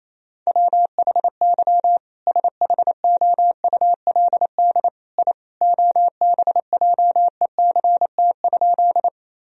Morse code, heard as a keyed tone, about 28 wpm, 700 Hz. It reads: WHY SHOULD I OBJECT?
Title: Sentences from Top 1000 Words - Review Part 3 - 28wpm